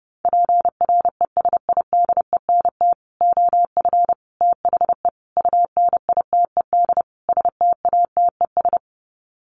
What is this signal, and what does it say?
Morse code, heard as a keyed tone, about 29 wpm, 700 Hz. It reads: PRESIDENT OF THE UNITED STATES
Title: Sets of 5 Words 30wpm